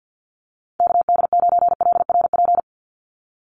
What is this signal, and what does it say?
Morse code, heard as a keyed tone, about 50 wpm, 700 Hz. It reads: KD9LRP
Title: USA Call Signs 50wpm